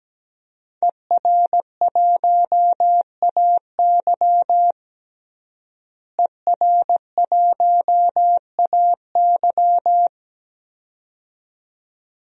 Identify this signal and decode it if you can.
Morse code, heard as a keyed tone, about 17 wpm, 700 Hz. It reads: ER1AY ER1AY
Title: DXCC-100 Call Signs - 5 Characters - Repeated 2x - 17wpm